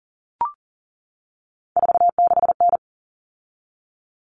Morse code, heard as a keyed tone, about 40 wpm, 700 Hz. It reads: 46N
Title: Common Cut Numbers - 3-Digit 40wpm